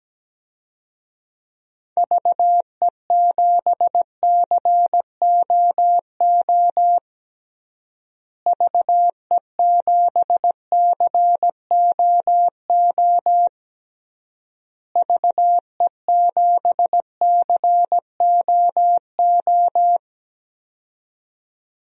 Morse code, heard as a keyed tone, about 17 wpm, 700 Hz. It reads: VE7COO VE7COO VE7COO
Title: DXCC-100 Call Signs - 6 Characters - Repeated 3x - 17wpm